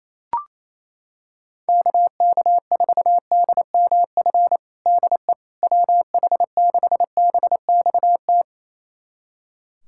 Morse code, heard as a keyed tone, about 27 wpm, 700 Hz. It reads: KK4DMF DE WH6BXT
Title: USA Callsign DE Callsign 28wpm